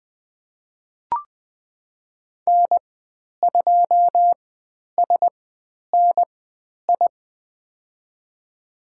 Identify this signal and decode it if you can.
Morse code, heard as a keyed tone, about 20 wpm, spaced as for 10 wpm, 700 Hz. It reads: N2SNI
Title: 095 Lesson 16g - Letter c - Call Signs 20wpm